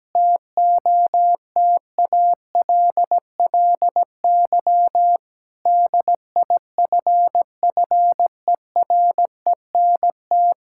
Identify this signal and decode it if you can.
Morse code, heard as a keyed tone, about 17 wpm, 700 Hz. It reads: TOTALLY DIFFERENT
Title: Sentences from Top 1600 Words 17wpm